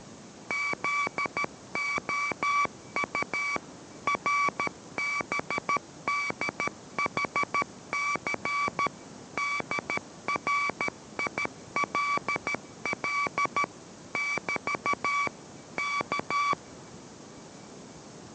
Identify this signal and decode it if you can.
Morse code, heard as a keyed tone, about 13 wpm, 2300 Hz, with noise. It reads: ZOU RBDHC DRILL <BT> K